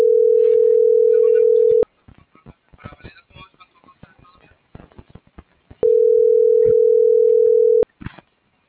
Ringback tone (North America), 440 + 480 Hz.